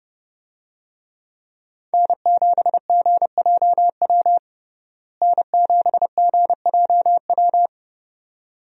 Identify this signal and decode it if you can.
Morse code, heard as a keyed tone, about 32 wpm, 700 Hz. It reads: N7GJW N7GJW
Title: USA Call Signs - 5 Characters - Repeated 2x - 30wpm